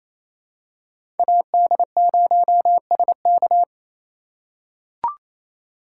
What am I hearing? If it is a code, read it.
Morse code, heard as a keyed tone, about 28 wpm, 700 Hz. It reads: AD0SK
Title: USA Call Signs - 5 Characters - 4x Character Spacing - 28wpm